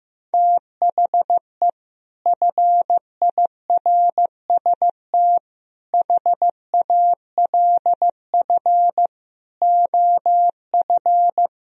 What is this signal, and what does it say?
Morse code, heard as a keyed tone, about 15 wpm, 700 Hz. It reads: THE FIRST HALF OF
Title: Speed-Racing - Sets of 5 Words 15wpm